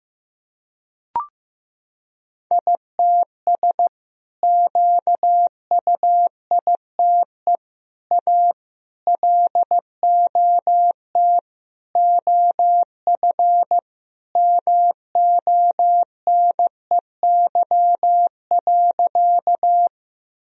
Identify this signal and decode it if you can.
Morse code, heard as a keyed tone, about 15 wpm, 700 Hz. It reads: ITS QUITE A LOT OF MONEY.
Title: Sentences from Top 1000 Words - Review Part 3 - 15wpm